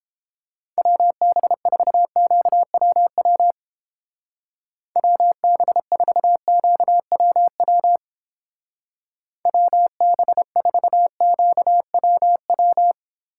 Morse code, heard as a keyed tone, about 30 wpm, 700 Hz. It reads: WB4QWW WB4QWW WB4QWW